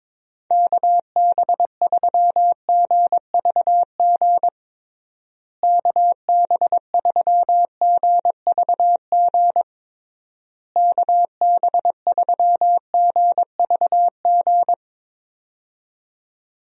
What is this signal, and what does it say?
Morse code, heard as a keyed tone, about 22 wpm, 700 Hz. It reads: KB3GVG KB3GVG KB3GVG
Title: USA Call Signs - 6 Characters - Repeated 3x - 22wpm